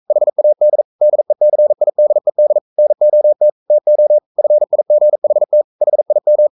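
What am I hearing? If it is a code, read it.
Morse code, heard as a keyed tone, about 42 wpm, 600 Hz. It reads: HAD DECIDED NOT TO FIGHT HIM
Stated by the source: Morse code sending of an English novel at 42 wpm.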